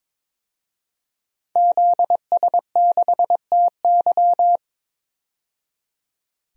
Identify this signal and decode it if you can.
Morse code, heard as a keyed tone, about 22 wpm, 700 Hz. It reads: ZS6TY
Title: DXCC-100 Call Signs - 5 Characters - 22wpm